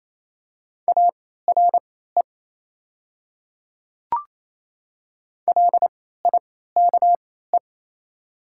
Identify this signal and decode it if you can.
Morse code, heard as a keyed tone, about 28 wpm, spaced as for 16 wpm, 700 Hz. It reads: ARE LIKE
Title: 3x Character Spacing - Top 100 Words - 28wpm